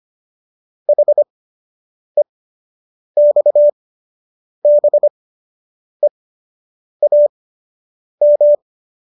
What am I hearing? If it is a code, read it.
Morse code, heard as a keyed tone, about 25 wpm, spaced as for 8 wpm, 590 Hz. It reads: HEXBEAM